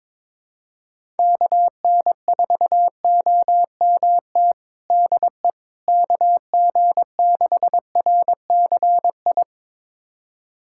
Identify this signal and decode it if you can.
Morse code, heard as a keyed tone, about 22 wpm, 700 Hz. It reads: KN4OMT DE KG6RCI
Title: USA Callsign DE Callsign - 6 Characters - 22wpm